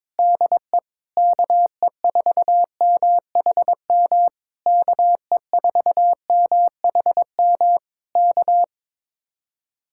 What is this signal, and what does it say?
Morse code, heard as a keyed tone, about 22 wpm, 700 Hz. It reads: DE KE4MHM KE4MHM K
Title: Speed-Racing Common QSO Phrases - 22wpm